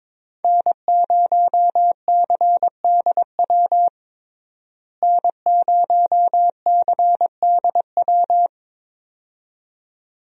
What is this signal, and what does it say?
Morse code, heard as a keyed tone, about 21 wpm, 700 Hz. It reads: N0CDW N0CDW
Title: USA Call Signs - 5 Characters - Repeated 2x - 22wpm